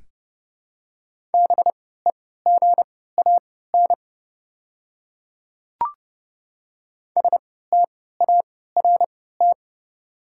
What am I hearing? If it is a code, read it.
Morse code, heard as a keyed tone, about 30 wpm, spaced as for 17 wpm, 700 Hz. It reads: BEGAN START
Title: 3x Character Spacing - Top 500 Words - 30wpm